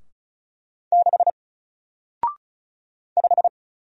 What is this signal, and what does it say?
Morse code, heard as a keyed tone, about 36 wpm, 700 Hz. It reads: 6 5
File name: Single-Letter-Number-35wpm